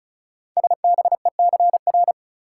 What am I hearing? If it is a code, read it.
Morse code, heard as a keyed tone, about 36 wpm, 690 Hz. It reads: SBECR
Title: Code Groups - 5 Letters - 35wpm